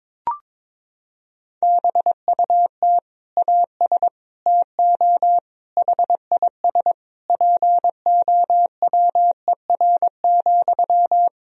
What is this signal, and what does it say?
Morse code, heard as a keyed tone, about 22 wpm, 700 Hz. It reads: BUT AS TO HIS POWER,
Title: Sentences from Top 1000 Words - Review Part 2 - 22wpm